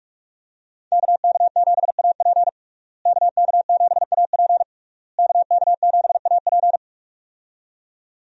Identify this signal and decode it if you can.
Morse code, heard as a keyed tone, about 46 wpm, 700 Hz. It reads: KK7AP KK7AP KK7AP
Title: USA Call Signs - 5 Characters - Repeated 3x - 45wpm